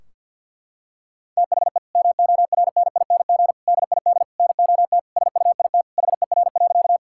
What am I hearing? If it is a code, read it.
Morse code, heard as a keyed tone, about 50 wpm, 700 Hz. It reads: THE MORNING DID NOT SUIT HER.
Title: Sentences from Top 1600 Words 50wpm